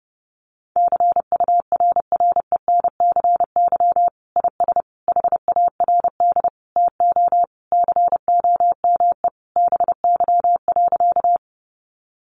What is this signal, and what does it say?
Morse code, heard as a keyed tone, about 30 wpm, 700 Hz. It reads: CURRENCY IS HARD TO COME BY.
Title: Sentences from Top 1500 Words 30wpm